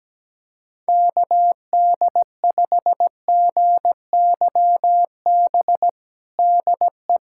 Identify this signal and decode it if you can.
Morse code, heard as a keyed tone, about 17 wpm, 700 Hz. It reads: KD5GYB DE
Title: USA Callsign DE Callsign - 6 Characters - 17wpm